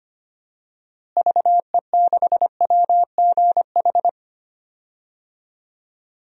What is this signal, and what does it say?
Morse code, heard as a keyed tone, about 25 wpm, 700 Hz. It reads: VE6WGH
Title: DXCC-100 Call Signs - 6 Characters - Repeated 2x - 25wpm